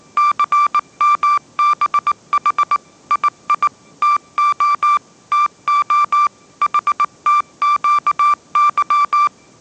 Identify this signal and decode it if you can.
Morse code, heard as a keyed tone, about 19 wpm, 1180 Hz. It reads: CMBH II TO TO HTQY
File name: mrx083 20wpm